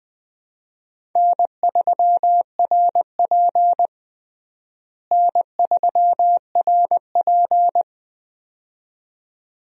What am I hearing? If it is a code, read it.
Morse code, heard as a keyed tone, about 21 wpm, 700 Hz. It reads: N3RP N3RP
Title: USA Call Signs - 4 Characters - Repeated 2x - 20wpm